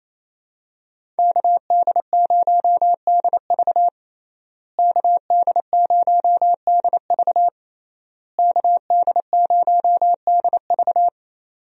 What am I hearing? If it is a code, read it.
Morse code, heard as a keyed tone, about 28 wpm, 700 Hz. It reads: KD0DV KD0DV KD0DV
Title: USA Call Signs - 5 Characters - Repeated 3x - 28wpm